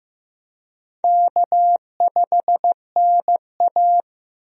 Morse code, heard as a keyed tone, about 15 wpm, 700 Hz. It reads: K5NA
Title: USA Call Signs - 4 Characters - Repeated 3x - 15wpm